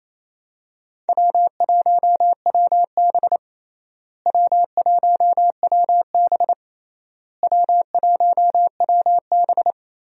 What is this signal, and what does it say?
Morse code, heard as a keyed tone, about 28 wpm, 700 Hz. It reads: W1WB W1WB W1WB
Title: USA Call Signs - 4 Characters - Repeated 3x - 28wpm